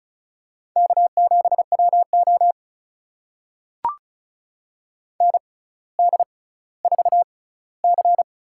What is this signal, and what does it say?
Morse code, heard as a keyed tone, about 36 wpm, spaced as for 18 wpm, 700 Hz. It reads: K7WO ND4C
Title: USA Call Signs - 4 Characters - 6x Character Spacing - 35wpm